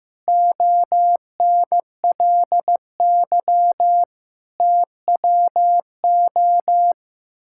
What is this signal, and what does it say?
Morse code, heard as a keyed tone, about 15 wpm, 700 Hz. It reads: ONLY TWO
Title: Sentences from Top 1000 Words - Review Part 6 - 15wpm